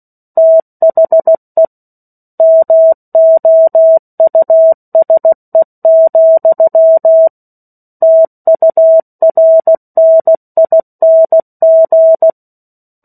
Morse code, heard as a keyed tone, about 16 wpm, 650 Hz. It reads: THE MOUSE, TURNING
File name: 03-Chapter-03 16 WPM